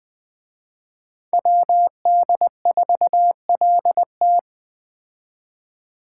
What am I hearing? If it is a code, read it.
Morse code, heard as a keyed tone, about 20 wpm, 700 Hz. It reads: WD4LT
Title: Speed Racing - USA Call Signs 17wpm